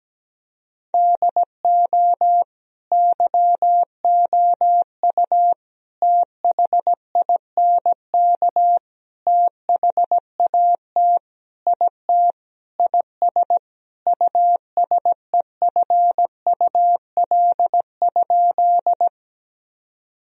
Morse code, heard as a keyed tone, about 17 wpm, 700 Hz. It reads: DO YOU THINK THAT IT IS USEFUL?
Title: Sentences from Top 1000 Words - Review Part 4 - 17wpm